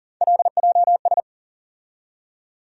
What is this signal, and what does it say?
Morse code, heard as a keyed tone, about 40 wpm, 700 Hz. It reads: LJS
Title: USA Call Sign Postfixes - 3 Letters 40wpm